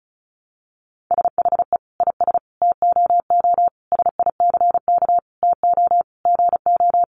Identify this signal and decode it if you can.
Morse code, heard as a keyed tone, about 34 wpm, 700 Hz. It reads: SHE IS TOO SICK TO GO
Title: Easy Phrases 35wpm